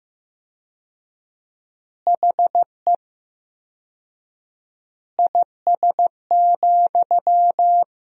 Morse code, heard as a keyed tone, about 16 wpm, 700 Hz. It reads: HE IS,
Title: 4x Word-Spacing - Sentences from Top 100 Words - 15wpm